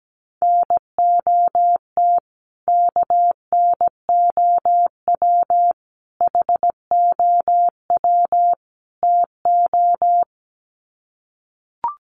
Morse code, heard as a keyed tone, about 17 wpm, 700 Hz. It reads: NOT KNOW HOW TO
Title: Sets of 4 Words 17wpm